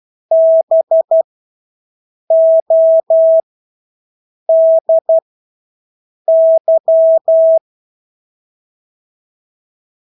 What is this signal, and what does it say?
Morse code, heard as a keyed tone, about 12 wpm, spaced as for 6 wpm, 650 Hz. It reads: BODY
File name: Engelske ord 12-6 wpm